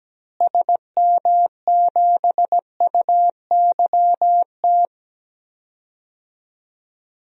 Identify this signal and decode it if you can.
Morse code, heard as a keyed tone, about 17 wpm, 700 Hz. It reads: SM7UYT
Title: DXCC-100 Call Signs - 6 Characters - Repeated 2x - 17wpm